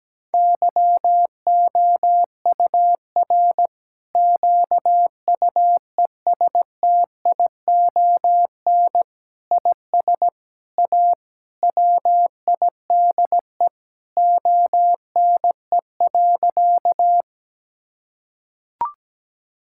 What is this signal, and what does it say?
Morse code, heard as a keyed tone, about 17 wpm, 700 Hz. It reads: YOUR QUESTION IS A WIDE ONE.
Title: Sentences from Top 1000 Words - Review Part 6 - 17wpm